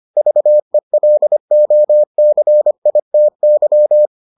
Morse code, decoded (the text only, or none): VELOCITY